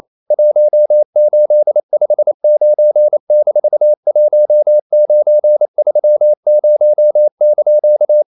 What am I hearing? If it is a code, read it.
Morse code, heard as a keyed tone, about 28 wpm, 600 Hz. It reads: 1859-1930)
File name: CW / Practice / EnglishQuotations0228